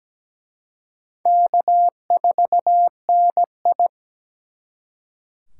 Morse code, heard as a keyed tone, about 20 wpm, 700 Hz. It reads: K4NI
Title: Speed Racing - USA 1x2 Call Signs 17wpm